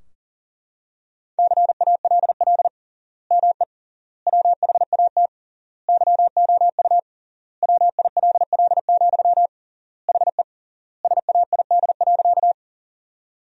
Morse code, heard as a keyed tone, about 37 wpm, 700 Hz. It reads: CALL ME WHAT YOU WILL, HE SAID.
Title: 3x Word-Spacing - Sentences from Top 200 Words - 40wpm